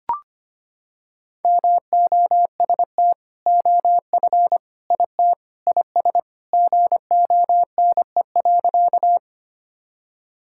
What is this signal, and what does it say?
Morse code, heard as a keyed tone, about 25 wpm, 700 Hz. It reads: MOST OF IT IS GONE.
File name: practice / 25wpm / Sentences with 2-Word Phrases - 25wpm